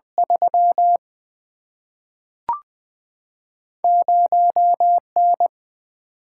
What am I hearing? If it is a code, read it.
Morse code, heard as a keyed tone, about 20 wpm, 700 Hz. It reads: 3 0 N
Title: Single-Letter Single-Number 20wpm